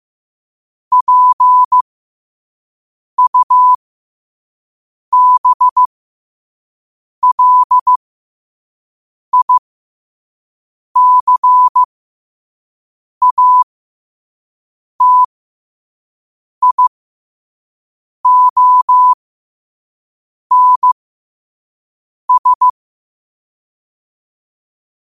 Morse code, heard as a keyed tone, about 15 wpm, spaced as for 5 wpm, 1000 Hz. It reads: PUBLICATIONS
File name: qotdfiles / 5wpm